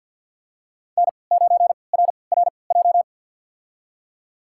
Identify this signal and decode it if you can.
Morse code, heard as a keyed tone, about 51 wpm, spaced as for 28 wpm, 700 Hz. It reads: N9RRJ
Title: USA Call Signs - 3x Character Spacing - 50wpm